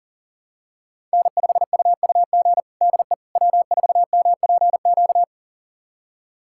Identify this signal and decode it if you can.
Morse code, heard as a keyed tone, about 40 wpm, 700 Hz. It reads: N5UUG DE W4MPQ
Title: USA Callsign DE Callsign - 5 Characters - 40wpm